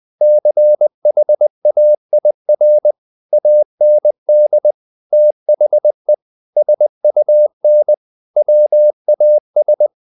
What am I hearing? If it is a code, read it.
Morse code, heard as a keyed tone, about 20 wpm, 600 Hz. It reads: CHAIR AND THE SUN WAS
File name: the old man and the sea / the old man and the sea part 1 20wpm